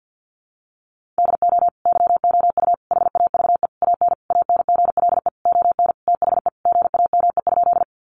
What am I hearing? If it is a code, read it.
Morse code, heard as a keyed tone, about 50 wpm, 700 Hz. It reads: DO YOU HAVE AN ANGLE ON THE GAME?
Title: Sentences from Top 1500 Words 50wpm